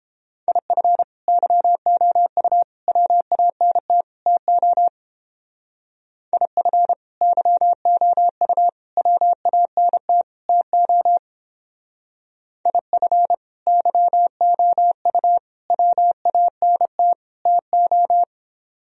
Morse code, heard as a keyed tone, about 30 wpm, 700 Hz. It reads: IF YOU WANT TO IF YOU WANT TO IF YOU WANT TO